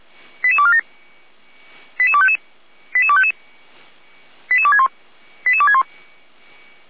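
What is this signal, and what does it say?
Five-tone selective call (ZVEI), heard as quick runs of five tones, tones of about 70 ms. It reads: 88247 88255 88255 88261 88261
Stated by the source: Five-tone ZVEI selective call received on SDR.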